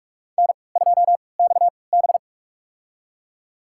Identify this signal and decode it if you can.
Morse code, heard as a keyed tone, about 46 wpm, spaced as for 25 wpm, 700 Hz. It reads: N2XB
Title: USA Call Signs - 4 Characters - 3x Character Spacing - 45wpm